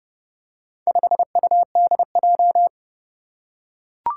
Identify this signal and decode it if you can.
Morse code, heard as a keyed tone, about 28 wpm, 700 Hz. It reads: 5UDJ
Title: Four Challenging Letter-Number Combinations 30wpm